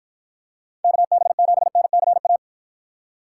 Morse code, heard as a keyed tone, about 50 wpm, 700 Hz. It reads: KB7NCA